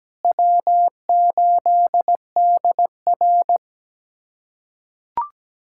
Morse code, heard as a keyed tone, about 17 wpm, 700 Hz. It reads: W8DR
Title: USA Callsign DE Callsign - 4 Characters - 17wpm